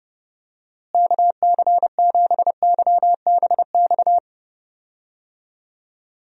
Morse code, heard as a keyed tone, about 31 wpm, 700 Hz. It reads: KC7YBX
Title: USA Call Signs - 6 Characters - Repeated 3x - 30wpm